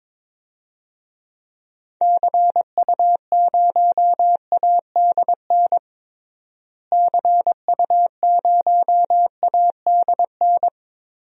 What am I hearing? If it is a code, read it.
Morse code, heard as a keyed tone, about 21 wpm, 700 Hz. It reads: CU0ADN CU0ADN
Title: DXCC-100 Call Signs - 6 Characters - Repeated 3x - 22wpm